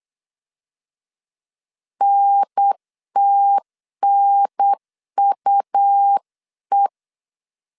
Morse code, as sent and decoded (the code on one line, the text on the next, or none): -. - -. ..- .
NTNUE